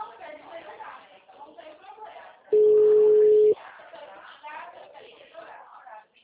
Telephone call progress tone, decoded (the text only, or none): ringback tone